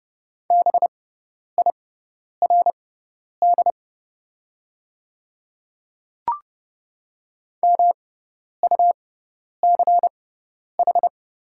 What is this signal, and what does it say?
Morse code, heard as a keyed tone, about 30 wpm, spaced as for 10 wpm, 700 Hz. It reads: BIRD MUCH
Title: 6x Character Spacing - Top 500 Words - 30wpm